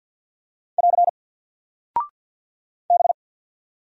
Morse code, heard as a keyed tone, about 50 wpm, 700 Hz. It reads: <AR> B
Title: Single Character (Letter:Number:Punctuation:Prosign) 50wpm